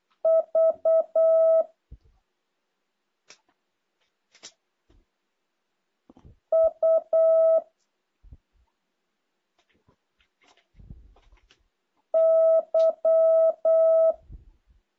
Morse code, heard as a keyed tone, about 8 wpm, 650 Hz. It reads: V U Y
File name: jh8cbh / 2023cw